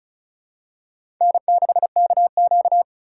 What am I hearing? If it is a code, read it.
Morse code, heard as a keyed tone, about 36 wpm, 700 Hz. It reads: N6KQ
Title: USA Call Signs - 4 Characters - 2x Character Spacing - 35wpm